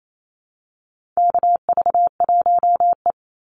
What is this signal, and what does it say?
Morse code, heard as a keyed tone, about 28 wpm, 700 Hz. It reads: KV1E
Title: Call Signs Repeated 2x - 28wpm